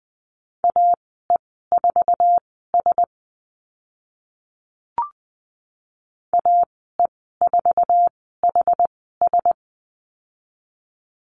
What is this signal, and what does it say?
Morse code, heard as a keyed tone, about 20 wpm, spaced as for 15 wpm, 700 Hz. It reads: AE4S AE4HS
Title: 069 Lesson 12g - Letter d - Call Signs 20wpm